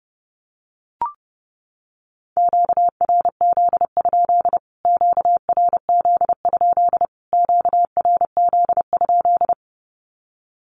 Morse code, heard as a keyed tone, about 30 wpm, 700 Hz. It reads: QRZ? QRZ? QRZ?
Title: Q-Codes Repeated 3x 30wpm